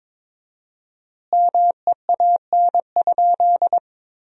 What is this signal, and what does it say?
Morse code, heard as a keyed tone, about 22 wpm, 700 Hz. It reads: MEAN?
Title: 4x Word-Spacing - Sentences from Top 200 Words - 22wpm